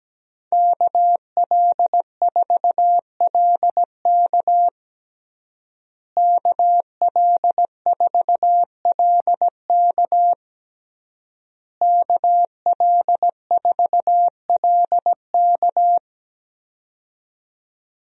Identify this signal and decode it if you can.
Morse code, heard as a keyed tone, about 17 wpm, 700 Hz. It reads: KL4LK KL4LK KL4LK
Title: USA Call Signs - 5 Characters - Repeated 3x - 17wpm